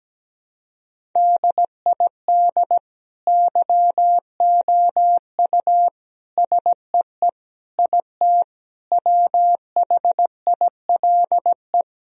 Morse code, heard as a keyed tone, about 17 wpm, 700 Hz. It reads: DID YOU SEE IT WHILE
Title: Sentences from Top 1000 Words - Review Part 1 - 17wpm